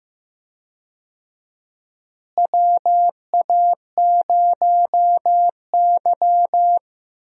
Morse code, heard as a keyed tone, about 15 wpm, 700 Hz. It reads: WA0Y
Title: USA Call Signs - 4 Characters - Repeated 3x - 15wpm